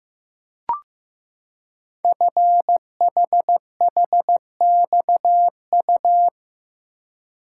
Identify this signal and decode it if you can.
Morse code, heard as a keyed tone, about 15 wpm, 700 Hz. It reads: FHHXU
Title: Code Groups - 5 Letters - 15wpm